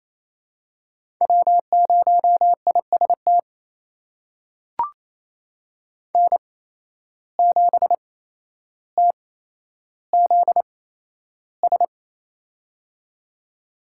Morse code, heard as a keyed tone, about 28 wpm, spaced as for 12 wpm, 700 Hz. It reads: W0IST N7TZS
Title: USA Call Signs - 5 Characters - 8x Character Spacing - 28wpm